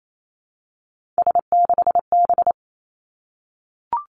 Morse code, heard as a keyed tone, about 28 wpm, 690 Hz. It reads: S6B
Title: Three Challenging Letter-Number Combinations 28wpm